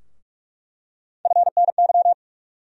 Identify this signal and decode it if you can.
Morse code, heard as a keyed tone, about 46 wpm, 700 Hz. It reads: FNY
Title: USA Call Sign Postfixes - 3 Letters 45wpm